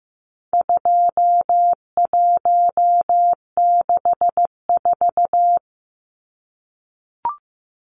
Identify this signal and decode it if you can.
Morse code, heard as a keyed tone, about 15 wpm, 700 Hz. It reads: 2164
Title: Numbers - 4-Digit 15wpm